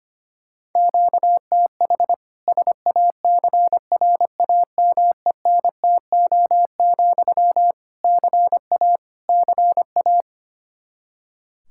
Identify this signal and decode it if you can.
Morse code, heard as a keyed tone, about 25 wpm, 700 Hz. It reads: QTH SACRAMENTO, CA CA
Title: K1USN SST QSO Practice 25wpm